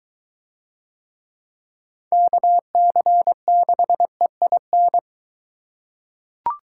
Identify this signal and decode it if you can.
Morse code, heard as a keyed tone, about 23 wpm, 700 Hz. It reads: KC6EIN